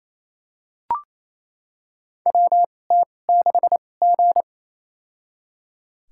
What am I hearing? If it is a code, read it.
Morse code, heard as a keyed tone, about 28 wpm, spaced as for 20 wpm, 690 Hz. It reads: WT6G
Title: USA Call Signs - 4 Characters - 2x Character Spacing - 28wpm